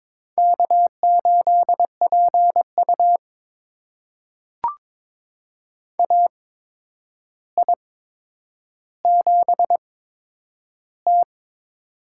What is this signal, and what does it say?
Morse code, heard as a keyed tone, about 23 wpm, spaced as for 10 wpm, 700 Hz. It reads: K8PU AI7T
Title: USA Call Signs - 4 Characters - 8x Character Spacing - 22wpm